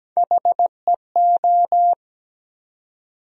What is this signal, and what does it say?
Morse code, heard as a keyed tone, about 17 wpm, 700 Hz. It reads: HEO
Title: USA Call Sign Postfixes - 3 Letters 17wpm